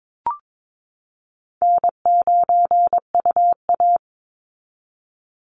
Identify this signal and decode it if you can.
Morse code, heard as a keyed tone, about 22 wpm, 700 Hz. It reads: N9UA
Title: Easy Call Signs Repeated 2x - 22wpm